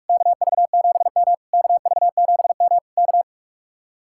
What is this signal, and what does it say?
Morse code, heard as a keyed tone, about 59 wpm, 700 Hz. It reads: KV7M KV7M K